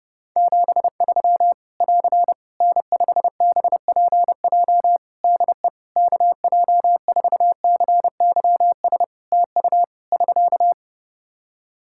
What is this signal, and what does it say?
Morse code, heard as a keyed tone, about 30 wpm, 700 Hz. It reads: 73 <AR> N5BPJ DE KJ4CYS TU <SK>
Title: Practice QSO - Exchange 4 - Ending - 30wpm